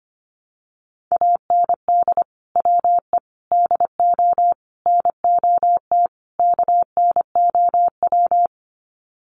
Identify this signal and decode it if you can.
Morse code, heard as a keyed tone, about 25 wpm, 700 Hz. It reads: AND WE DO NOT KNOW
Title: Sets of 5 Words Encore 25wpm